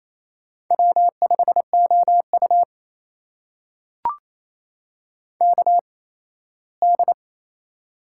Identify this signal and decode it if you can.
Morse code, heard as a keyed tone, about 28 wpm, spaced as for 17 wpm, 700 Hz. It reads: W5OU KD
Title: USA Call Signs - 4 Characters - 8x Character Spacing - 28wpm